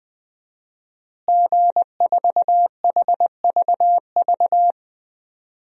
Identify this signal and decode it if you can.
Morse code, heard as a keyed tone, about 20 wpm, 700 Hz. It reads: G4HVV
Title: DXCC-100 Call Signs - 5 Characters - Repeated 3x - 20wpm